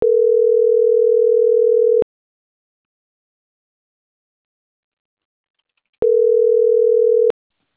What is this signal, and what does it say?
Ringback tone (North America), 440 + 480 Hz.
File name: CENTRAL MAYAB / out vanessa.loyola